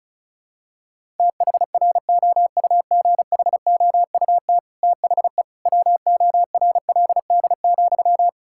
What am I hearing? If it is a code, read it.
Morse code, heard as a keyed tone, about 34 wpm, 700 Hz. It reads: THROUGHOUT THE WORLD,
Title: Sentences from Top 1000 Words - Review Part 5 - 35wpm